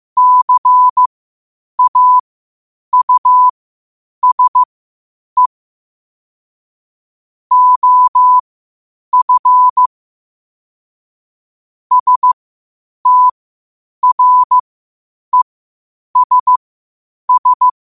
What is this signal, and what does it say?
Morse code, heard as a keyed tone, about 15 wpm, spaced as for 8 wpm, 1000 Hz. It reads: CAUSE OF STRESS
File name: qotdfiles / 8wpm